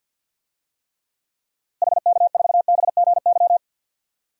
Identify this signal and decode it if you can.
Morse code, heard as a keyed tone, about 53 wpm, 700 Hz. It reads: HK4BGY
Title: DXCC-100 Call Signs - 6 Characters - Repeated 3x - 50wpm